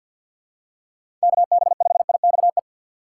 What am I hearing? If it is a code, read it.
Morse code, heard as a keyed tone, about 50 wpm, 700 Hz. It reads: KB5IXE
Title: USA Call Signs - 6 Characters - 4x Character Spacing - 50wpm